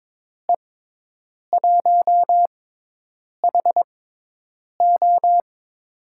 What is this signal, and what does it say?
Morse code, heard as a keyed tone, about 22 wpm, spaced as for 8 wpm, 700 Hz. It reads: E1HO
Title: USA Call Signs - 5 Characters - 6x Character Spacing - 22wpm